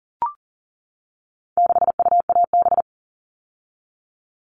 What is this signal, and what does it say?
Morse code, heard as a keyed tone, about 40 wpm, 700 Hz. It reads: 6UAB
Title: All Cut Numbers - 4-Digit 40wpm